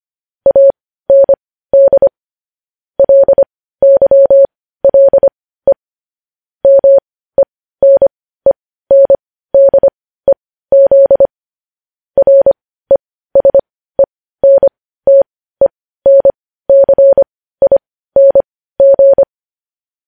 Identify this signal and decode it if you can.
Morse code, heard as a keyed tone, about 25 wpm, spaced as for 15 wpm, 550 Hz. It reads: AND LYLE MENENDEZ RESENTENCING
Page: news-headlines-in-morse-code-at-15-wpm